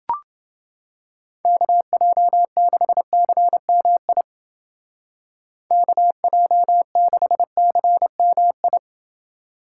Morse code, heard as a keyed tone, about 29 wpm, 700 Hz. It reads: KJ6CMI KJ6CMI